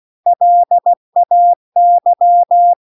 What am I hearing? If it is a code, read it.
Morse code, heard as a keyed tone, about 16 wpm, 700 Hz. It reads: LAY